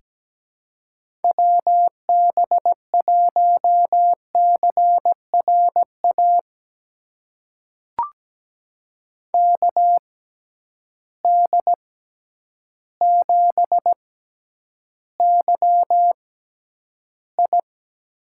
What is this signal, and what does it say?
Morse code, heard as a keyed tone, about 17 wpm, spaced as for 9 wpm, 700 Hz. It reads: WB1CRA KD7YI
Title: USA Call Signs - 6 Characters - 6x Character Spacing - 17wpm